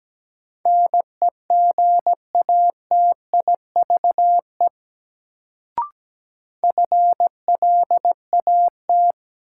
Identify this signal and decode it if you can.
Morse code, heard as a keyed tone, about 17 wpm, 700 Hz. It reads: NEGATIVE FLAT
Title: Sentences from Top 1000 Words - New Word Drill - 17wpm